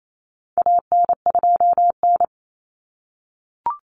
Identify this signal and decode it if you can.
Morse code, heard as a keyed tone, about 28 wpm, 700 Hz. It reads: AN2N
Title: Common Cut Numbers - 4-Digit 28wpm